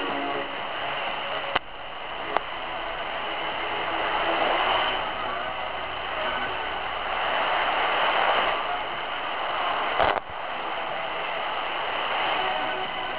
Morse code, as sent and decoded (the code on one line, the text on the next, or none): none